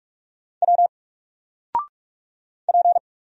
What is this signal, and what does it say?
Morse code, heard as a keyed tone, about 46 wpm, 700 Hz. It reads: W P
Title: Single Letter-Number 45wpm